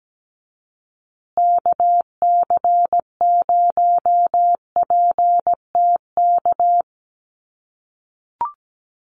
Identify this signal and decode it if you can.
Morse code, heard as a keyed tone, about 17 wpm, 700 Hz. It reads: KC0PTK
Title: USA Call Signs Repeated 2x 17wpm